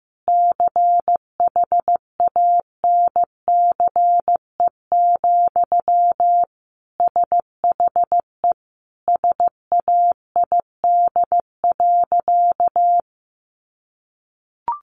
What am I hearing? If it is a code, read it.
Morse code, heard as a keyed tone, about 15 wpm, 700 Hz. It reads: CHANCE, SHE SAID.
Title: Sentences from Top 800 Words 15wpm